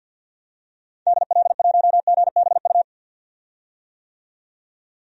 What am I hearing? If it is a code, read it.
Morse code, heard as a keyed tone, about 48 wpm, 700 Hz. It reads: DL1GBU
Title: DXCC-100 Call Signs - 6 Characters - 50wpm